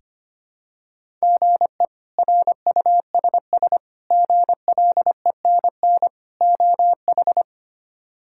Morse code, heard as a keyed tone, about 25 wpm, 700 Hz. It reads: GE RUSS GLENN OH
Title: K1USN SST QSO Practice - 25wpm